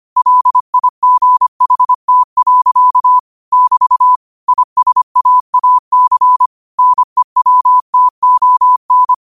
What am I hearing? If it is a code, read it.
Morse code, heard as a keyed tone, about 25 wpm, 1000 Hz. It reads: LIGHT. <BT> ISAAC NEWTON